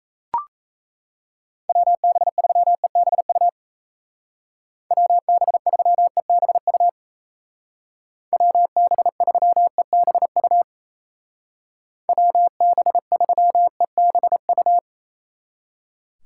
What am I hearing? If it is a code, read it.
Morse code, heard as a keyed tone, about 34 wpm, 700 Hz. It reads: WB3EBU WB3EBU WB3EBU WB3EBU
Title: Speed Racing - USA 2x3 Call Signs 28wpm